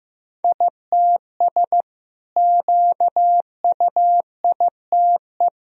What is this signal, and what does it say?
Morse code, heard as a keyed tone, about 15 wpm, 700 Hz. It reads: ITS QUITE
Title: Sentences from Top 1000 Words - Review Part 4 - 15wpm